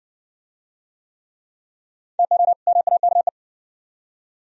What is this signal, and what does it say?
Morse code, heard as a keyed tone, about 60 wpm, 690 Hz. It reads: TO MAKE